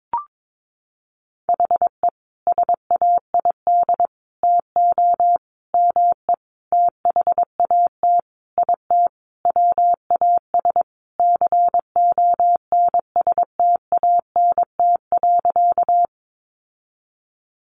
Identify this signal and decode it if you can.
Morse code, heard as a keyed tone, about 22 wpm, 700 Hz. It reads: HE SAID TO ME THAT IT WAS CONSTANT.
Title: Sentences from Top 1500 Words 22wpm